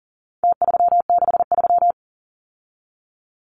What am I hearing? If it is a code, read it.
Morse code, heard as a keyed tone, about 39 wpm, 700 Hz. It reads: T363
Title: Numbers - Contest Style - 4-Digit 40wpm